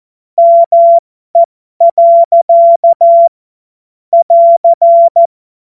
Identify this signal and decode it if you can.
Morse code, heard as a keyed tone, about 14 wpm, spaced as for 12 wpm, 680 Hz. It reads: ME. <AR>
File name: Morsekurs / Witze